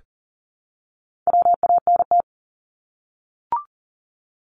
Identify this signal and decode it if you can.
Morse code, heard as a keyed tone, about 40 wpm, 700 Hz. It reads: WANT
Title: Top 200 Words Repeated 40wpm